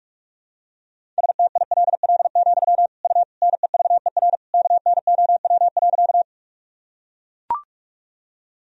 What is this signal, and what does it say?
Morse code, heard as a keyed tone, about 43 wpm, 700 Hz. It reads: STILL, U NEVER KNOW.
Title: Sentences from Top 300 Words and Abbreviations 45wpm